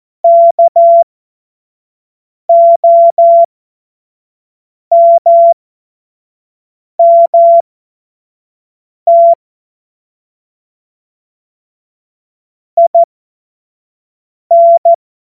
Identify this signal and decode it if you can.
Morse code, heard as a keyed tone, about 14 wpm, spaced as for 5 wpm, 680 Hz. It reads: KOMMT IN
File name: Witz-19-August-2025-25-F70